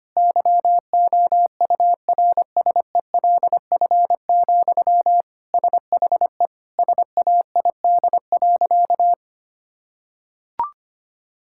Morse code, heard as a keyed tone, about 25 wpm, 700 Hz. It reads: YOURSELF, SHE SAID.
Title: Sentences from Top 1600 Words 25wpm